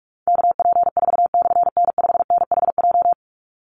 Morse code, heard as a keyed tone, about 44 wpm, 700 Hz. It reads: KP4/N5NHJ
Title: Call Signs Repeated 3x - 45wpm